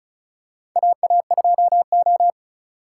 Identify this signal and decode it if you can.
Morse code, heard as a keyed tone, about 36 wpm, 700 Hz. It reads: AA2O
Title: USA Call Signs - 4 Characters - 4x Character Spacing - 35wpm